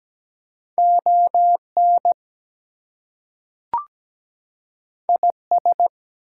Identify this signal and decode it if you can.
Morse code, heard as a keyed tone, about 17 wpm, 700 Hz. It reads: ON IS
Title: Two Letter Words - 17wpm